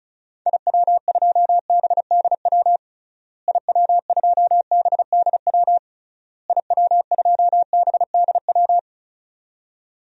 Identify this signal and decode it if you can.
Morse code, heard as a keyed tone, about 36 wpm, 700 Hz. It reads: IW2BDW IW2BDW IW2BDW
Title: DXCC-100 Call Signs - 6 Characters - Repeated 3x - 35wpm